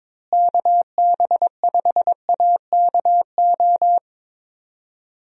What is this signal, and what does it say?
Morse code, heard as a keyed tone, about 22 wpm, 700 Hz. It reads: KB5AKO
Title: USA Call Signs - 6 Characters - Repeated 2x - 22wpm